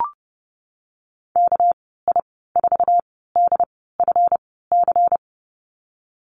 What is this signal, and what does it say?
Morse code, heard as a keyed tone, about 32 wpm, spaced as for 17 wpm, 700 Hz. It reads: KI4DFC
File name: Lesson 37g - Letter x - Call Signs 30wpm